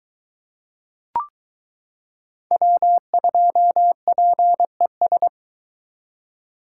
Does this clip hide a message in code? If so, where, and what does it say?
Yes: Morse code from 2.5 to 5.3 s, heard as a keyed tone, about 22 wpm, 700 Hz. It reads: W2PES